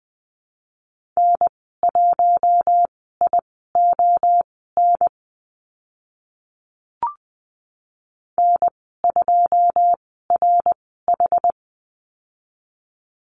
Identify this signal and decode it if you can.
Morse code, heard as a keyed tone, about 20 wpm, spaced as for 15 wpm, 700 Hz. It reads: N1ION N2RH
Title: 082 Lesson 14g - Number 2 - Call Signs 20wpm